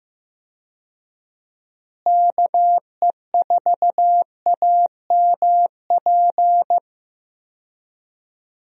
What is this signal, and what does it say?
Morse code, heard as a keyed tone, about 15 wpm, 700 Hz. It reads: KE4AMP
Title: USA Call Signs - 6 Characters - Repeated 2x - 15wpm